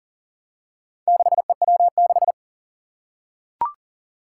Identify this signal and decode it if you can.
Morse code, heard as a keyed tone, about 40 wpm, 700 Hz. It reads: 6EW6
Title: Four Challenging Letter-Number Combinations 40wpm